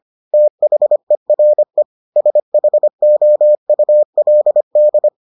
Morse code, heard as a keyed tone, about 25 wpm, 600 Hz. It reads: THERE SHOULD